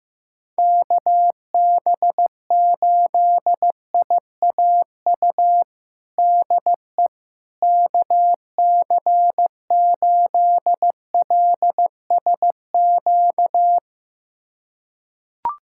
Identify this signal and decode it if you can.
Morse code, heard as a keyed tone, about 15 wpm, 700 Hz. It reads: KB8IAU DE KC8LSQ
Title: USA Callsign DE Callsign - 6 Characters - 15wpm